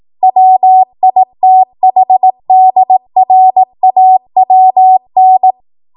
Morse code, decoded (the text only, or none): WITHDRAWN